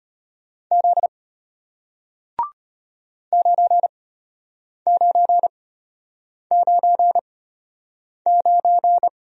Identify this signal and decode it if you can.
Morse code, heard as a keyed tone, about 34 wpm, spaced as for 8 wpm, 700 Hz. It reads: Z 9999